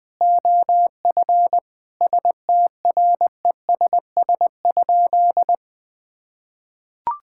Morse code, heard as a keyed tone, about 20 wpm, 700 Hz. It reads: OF STRESS?
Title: Sentences from Top 1000 Words - Review Part 6 - 20wpm